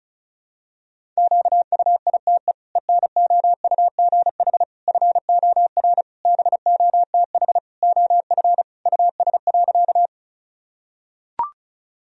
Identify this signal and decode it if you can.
Morse code, heard as a keyed tone, about 36 wpm, 700 Hz. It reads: QUITE ENOUGH FOR BOTH OF US.
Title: Sentences from Top 1000 Words - Review Part 1 - 35wpm